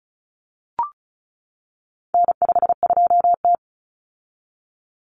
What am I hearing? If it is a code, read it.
Morse code, heard as a keyed tone, about 34 wpm, 700 Hz. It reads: N52T
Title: Common Cut Numbers - 4-Digit 35wpm